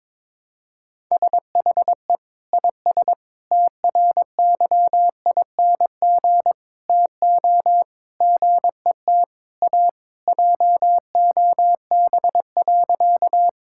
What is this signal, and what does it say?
Morse code, heard as a keyed tone, about 22 wpm, 700 Hz. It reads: SHE IS TRYING TO GET A JOB.
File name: practice / 22wpm / Sentences with 3-Word Phrases - 22wpm